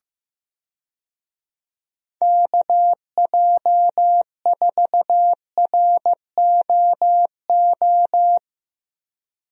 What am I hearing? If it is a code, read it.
Morse code, heard as a keyed tone, about 15 wpm, 700 Hz. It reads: KJ4ROO
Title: USA Call Signs - 6 Characters - 15wpm